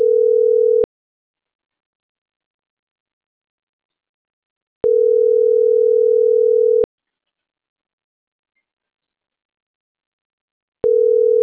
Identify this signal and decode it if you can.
Ringback tone (North America), 440 + 480 Hz.